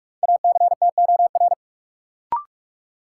Morse code, heard as a keyed tone, about 46 wpm, 700 Hz. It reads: ACTOR
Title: Sentences from Top 1500 Words - New Word Drill Alt - 45wpm